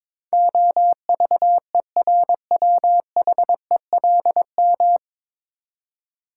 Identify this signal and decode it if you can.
Morse code, heard as a keyed tone, about 21 wpm, 700 Hz. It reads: OVERWHELM
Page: Morse Code Ninja practice 22wpm